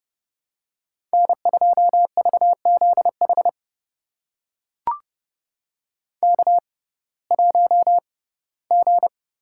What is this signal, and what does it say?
Morse code, heard as a keyed tone, about 30 wpm, spaced as for 18 wpm, 700 Hz. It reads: N2VZH K1G
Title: USA Call Signs - 5 Characters - 6x Character Spacing - 30wpm